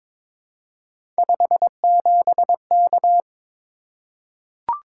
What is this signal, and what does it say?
Morse code, heard as a keyed tone, about 22 wpm, 700 Hz. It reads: 57K
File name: practice / 22wpm / Three Letter-Number Combinations - 22wpm